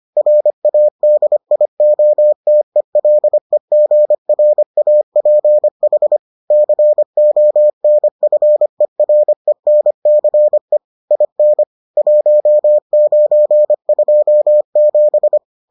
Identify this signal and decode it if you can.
Morse code, heard as a keyed tone, about 25 wpm, 600 Hz. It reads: RADIOTELEGRAPH CONFERENCE IN 1927